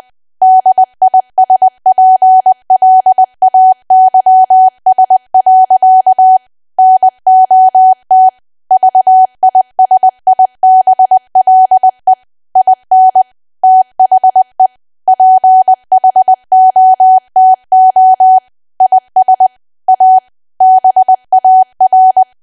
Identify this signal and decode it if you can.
Morse code, heard as a keyed tone, about 20 wpm, 750 Hz. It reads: DISPLAYS. NOT VISIBLE IN THE PHOTO IS A BAR
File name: cwa / W1AW-Level3 / 20WPM